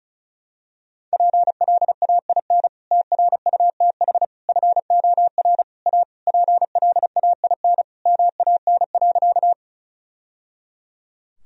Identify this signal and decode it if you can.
Morse code, heard as a keyed tone, about 34 wpm, 700 Hz. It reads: PLAIN TRUTH FOR A PLAIN MAN.
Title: Sentences from Top 1000 Words - Review Part 2 - 35wpm